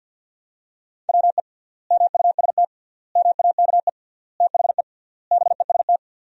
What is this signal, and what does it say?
Morse code, heard as a keyed tone, about 50 wpm, 700 Hz. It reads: WE MUST MAKE THE BEST
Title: 3x Word-Spacing - Sentences from Top 400 Words - 50wpm